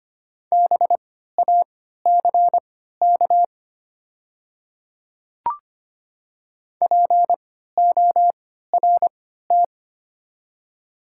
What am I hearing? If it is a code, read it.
Morse code, heard as a keyed tone, about 25 wpm, spaced as for 14 wpm, 700 Hz. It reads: BACK PORT
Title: 3x Character Spacing - Top 200 Words - 25wpm